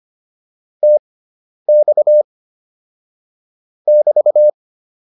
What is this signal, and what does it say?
Morse code, heard as a keyed tone, about 25 wpm, spaced as for 10 wpm, 600 Hz. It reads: TX <BT>